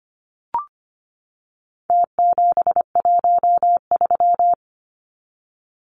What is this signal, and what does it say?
Morse code, heard as a keyed tone, about 25 wpm, 700 Hz. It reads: T713
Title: Numbers - Contest Style - 4-Digit 25wpm